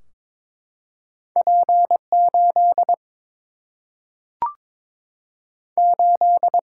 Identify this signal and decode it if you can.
Morse code, heard as a keyed tone, about 22 wpm, 700 Hz. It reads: P 8 8
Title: Single-Letter Single-Number 22wpm